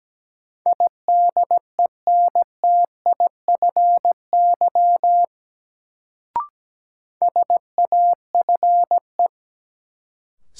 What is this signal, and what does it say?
Morse code, heard as a keyed tone, about 17 wpm, 700 Hz. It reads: IDENTIFY SAFE
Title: Sentences from Top 900 Words - New Word Drill - 17wpm